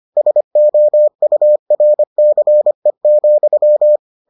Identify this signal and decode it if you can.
Morse code, heard as a keyed tone, about 25 wpm, 600 Hz. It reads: SOURCE,